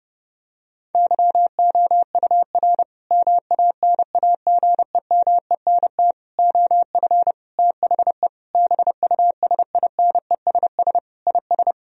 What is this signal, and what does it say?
Morse code, heard as a keyed tone, about 30 wpm, 700 Hz. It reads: YOUR MANAGEMENT OF THE BUSINESS IS
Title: Sentences from Top 1000 Words - Review Part 4 - 30wpm